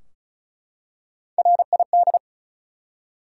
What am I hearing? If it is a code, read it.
Morse code, heard as a keyed tone, about 34 wpm, 700 Hz. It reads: RID
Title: Three Letter Words - 35wpm